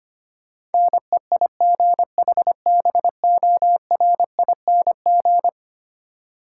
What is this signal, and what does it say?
Morse code, heard as a keyed tone, about 25 wpm, 700 Hz. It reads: NEIGHBORING
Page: morse-code-ninja-practice-25wpm